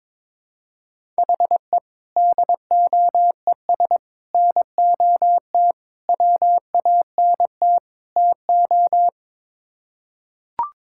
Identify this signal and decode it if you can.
Morse code, heard as a keyed tone, about 22 wpm, 700 Hz. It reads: HE DOES NOT WANT TO
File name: practice / 22wpm / Build A Phrase - 5-Word Phrases - 22wpm